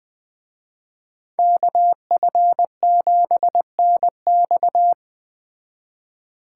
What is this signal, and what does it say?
Morse code, heard as a keyed tone, about 20 wpm, 700 Hz. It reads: KF7NX
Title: Speed Racing - USA Call Signs 17wpm